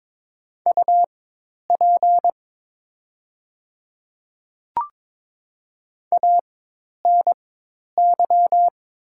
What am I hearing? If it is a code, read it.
Morse code, heard as a keyed tone, about 22 wpm, spaced as for 10 wpm, 700 Hz. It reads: UP ANY
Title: 4x Character Spacing - Top 200 Words - 22wpm